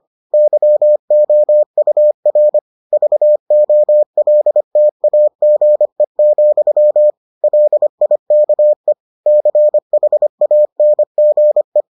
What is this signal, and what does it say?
Morse code, heard as a keyed tone, about 25 wpm, 600 Hz. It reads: YOUR VOLTAGE, LIKE CHANGE